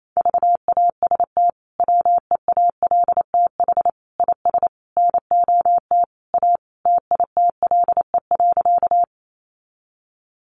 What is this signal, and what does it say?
Morse code, heard as a keyed tone, about 28 wpm, 700 Hz. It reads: VAST WEALTH IS NOT A TITLE.
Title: Sentences from Top 1300 Words 28wpm